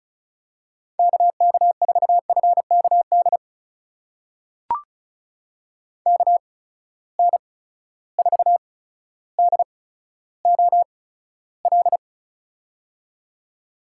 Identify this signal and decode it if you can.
Morse code, heard as a keyed tone, about 36 wpm, spaced as for 15 wpm, 700 Hz. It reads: KK4FKD KN4DOL
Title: USA Call Signs - 6 Characters - 8x Character Spacing - 35wpm